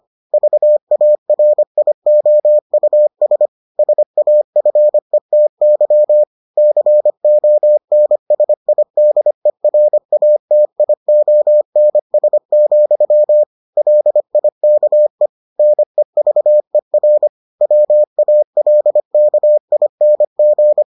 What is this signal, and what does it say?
Morse code, heard as a keyed tone, about 25 wpm, 600 Hz. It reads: VARIOUS SAFETY CONSIDERATIONS, LIKE NEVER WALKING